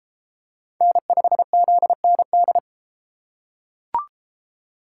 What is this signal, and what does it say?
Morse code, heard as a keyed tone, about 33 wpm, 700 Hz. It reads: N5ZND